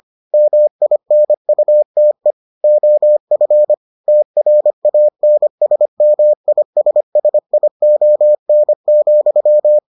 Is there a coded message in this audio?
Yes: Morse code from 0.3 to 9.8 s, heard as a keyed tone, about 25 wpm, 600 Hz. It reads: MINUTE OF TRANSMISSION,